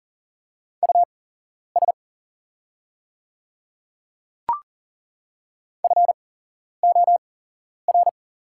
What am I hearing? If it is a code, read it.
Morse code, heard as a keyed tone, about 38 wpm, spaced as for 11 wpm, 690 Hz. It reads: US FOR
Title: 8x Character Spacing - Top 300 Words - 40wpm